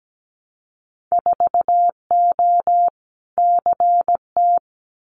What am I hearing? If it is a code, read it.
Morse code, heard as a keyed tone, about 17 wpm, 700 Hz. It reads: 4O CT
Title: Winter Field Day Exchanges Encore 17wpm